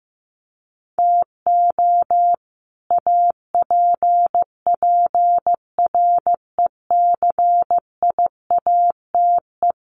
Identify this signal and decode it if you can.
Morse code, heard as a keyed tone, about 15 wpm, 700 Hz. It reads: TO APPRECIATE
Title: Sentences from Top 1300 Words 15wpm